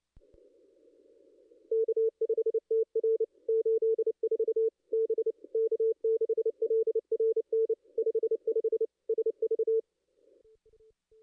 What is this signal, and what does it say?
Morse code, heard as a keyed tone, about 28 wpm, 440 Hz. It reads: K5TR 84 B K6LRN 55 SV
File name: contests / ss-cw-k5tr-op-n5rz -2200